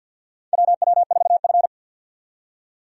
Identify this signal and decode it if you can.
Morse code, heard as a keyed tone, about 53 wpm, 700 Hz. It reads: WW4F
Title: USA Call Signs - 4 Characters - 50wpm